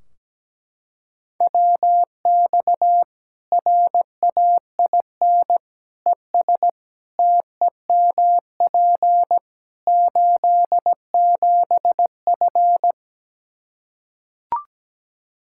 Morse code, heard as a keyed tone, about 17 wpm, 700 Hz. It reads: WX RAIN ES TEMP 87F
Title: Practice QSO - Exchange 2 - Info - 17wpm